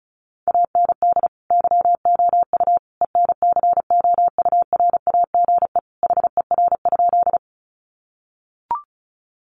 Morse code, heard as a keyed tone, about 34 wpm, 700 Hz. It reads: AND YOU ENCOURAGE HER?
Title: Sentences from Top 1400 Words 35wpm